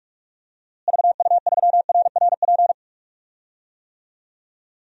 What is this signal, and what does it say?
Morse code, heard as a keyed tone, about 44 wpm, 700 Hz. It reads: VU3RRP
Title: DXCC-100 Call Signs - 6 Characters - Repeated 2x - 45wpm